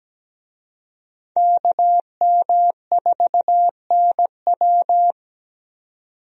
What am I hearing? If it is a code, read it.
Morse code, heard as a keyed tone, about 17 wpm, 700 Hz. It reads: KM4NW
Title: USA Call Signs - 5 Characters - 17wpm